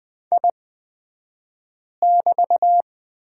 Morse code, heard as a keyed tone, about 20 wpm, spaced as for 5 wpm, 700 Hz. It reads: I<BT>
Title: 223 Lesson 36d - BT - Two Characters 20wpm